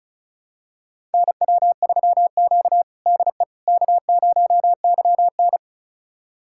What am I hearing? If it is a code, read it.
Morse code, heard as a keyed tone, about 36 wpm, 700 Hz. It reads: NW3Q DE K0YN
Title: USA Callsign DE Callsign - 4 Characters - 35wpm